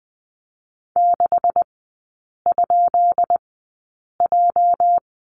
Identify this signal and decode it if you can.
Morse code, heard as a keyed tone, about 20 wpm, spaced as for 8 wpm, 700 Hz. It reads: 6?J
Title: 218 Lesson 35e - Number 0 - Three Characters 20wpm